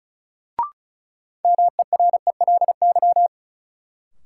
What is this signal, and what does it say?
Morse code, heard as a keyed tone, about 36 wpm, 700 Hz. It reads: MERELY
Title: Sentences from Top 1400 Words - New Word Drill Alt - 35wpm